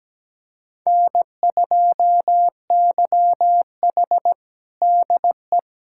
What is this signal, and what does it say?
Morse code, heard as a keyed tone, about 17 wpm, 700 Hz. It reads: N2YH DE
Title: USA Callsign DE Callsign - 4 Characters - 17wpm